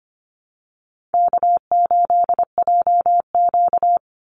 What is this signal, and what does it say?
Morse code, heard as a keyed tone, about 25 wpm, 700 Hz. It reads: K8JQ
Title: Call Signs Repeated 3x - 25wpm